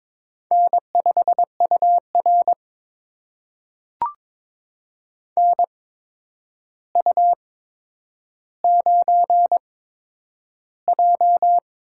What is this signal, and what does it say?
Morse code, heard as a keyed tone, about 22 wpm, spaced as for 9 wpm, 700 Hz. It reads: N5UR NU9J
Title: USA Call Signs - 4 Characters - 8x Character Spacing - 22wpm